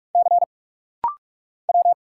Morse code, heard as a keyed tone, about 46 wpm, 700 Hz. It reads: C W
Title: Single Letters - Rapid-Fire 45wpm